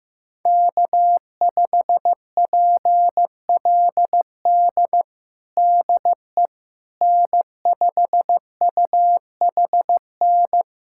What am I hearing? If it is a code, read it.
Morse code, heard as a keyed tone, about 16 wpm, 700 Hz. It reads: K5PLD DE N5UHN
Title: USA Callsign DE Callsign - 5 Characters - 15wpm